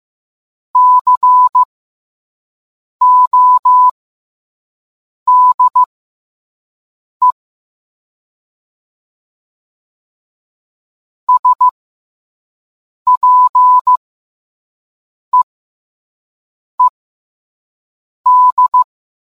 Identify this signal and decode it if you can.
Morse code, heard as a keyed tone, about 15 wpm, spaced as for 5 wpm, 1000 Hz. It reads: CODE SPEED